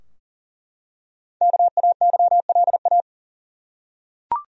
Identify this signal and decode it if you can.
Morse code, heard as a keyed tone, about 40 wpm, 700 Hz. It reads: KAYLA
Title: USA Common Names - 40wpm